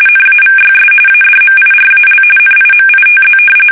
RTTY, about 45.45 baud. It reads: NAVYS ADMIRAL NIMITZ
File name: fsk850 / battleshipnj